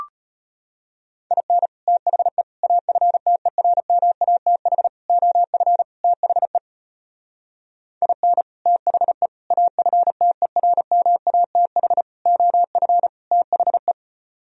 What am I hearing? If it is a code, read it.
Morse code, heard as a keyed tone, about 32 wpm, 700 Hz. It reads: IN THE AFTERMATH OF THE IN THE AFTERMATH OF THE